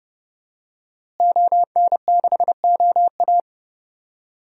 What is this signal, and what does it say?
Morse code, heard as a keyed tone, about 32 wpm, 700 Hz. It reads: ON6OA
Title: DXCC-100 Call Signs - 5 Characters - Repeated 3x - 30wpm